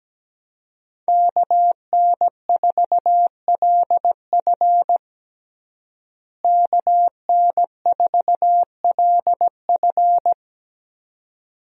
Morse code, heard as a keyed tone, about 17 wpm, 700 Hz. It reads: KN4LF KN4LF
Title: USA Call Signs - 5 Characters - Repeated 2x - 17wpm